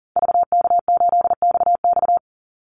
Morse code, heard as a keyed tone, about 40 wpm, 700 Hz. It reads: VK8XX
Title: Call Signs Repeated 2x - 40wpm